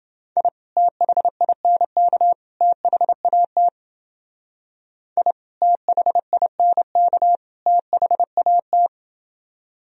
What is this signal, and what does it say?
Morse code, heard as a keyed tone, about 30 wpm, 700 Hz. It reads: I THINK THAT I THINK THAT